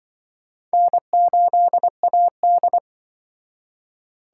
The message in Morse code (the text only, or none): N8AD